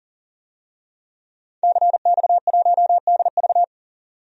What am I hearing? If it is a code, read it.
Morse code, heard as a keyed tone, about 39 wpm, 700 Hz. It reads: CX1DV
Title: DXCC-100 Call Signs - 5 Characters - 40wpm